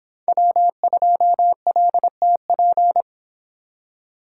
Morse code, heard as a keyed tone, about 24 wpm, 700 Hz. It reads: W2LTP